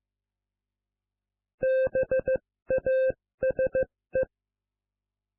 Morse code, heard as a keyed tone, about 15 wpm, spaced as for 13 wpm, 520 Hz. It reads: BASE